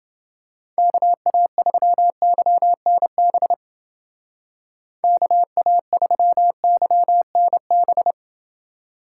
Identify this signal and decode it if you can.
Morse code, heard as a keyed tone, about 28 wpm, 700 Hz. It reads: KA3YNB KA3YNB